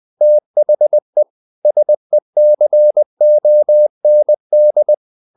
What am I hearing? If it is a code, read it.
Morse code, heard as a keyed tone, about 20 wpm, 600 Hz. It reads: THE SECOND
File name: the old man and the sea part 2 20wpm